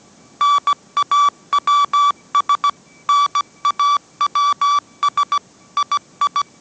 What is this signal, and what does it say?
Morse code, heard as a keyed tone, about 17 wpm, 1150 Hz. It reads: NAWS NAWS II